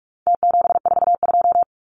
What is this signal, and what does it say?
Morse code, heard as a keyed tone, about 44 wpm, 700 Hz. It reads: T742
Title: Numbers - Contest Style - 4-Digit 45wpm